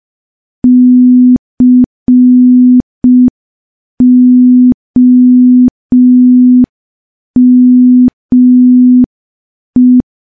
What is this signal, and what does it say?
Morse code, heard as a keyed tone, about 5 wpm, 250 Hz. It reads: COME
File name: morse / morse-training-05WPM